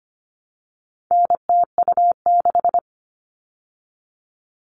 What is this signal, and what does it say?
Morse code, heard as a keyed tone, about 25 wpm, 700 Hz. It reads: NTU6
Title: All Cut Numbers - 4-Digit 25wpm